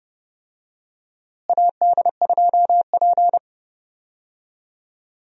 Morse code, heard as a keyed tone, about 29 wpm, 700 Hz. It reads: AD2P
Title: USA Call Signs - 4 Characters - Repeated 3x - 30wpm